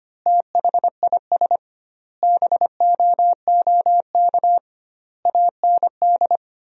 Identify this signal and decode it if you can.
Morse code, heard as a keyed tone, about 25 wpm, 700 Hz. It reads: THIS BOOK AND
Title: 2x Word-Spacing - Sentences from Top 400 Words - 25wpm